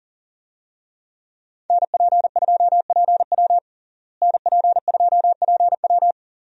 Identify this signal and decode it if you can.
Morse code, heard as a keyed tone, about 40 wpm, 700 Hz. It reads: NP2PW NP2PW
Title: USA Call Signs - 5 Characters - Repeated 3x - 40wpm